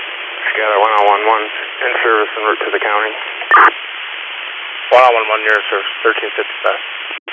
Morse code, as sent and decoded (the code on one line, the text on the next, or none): none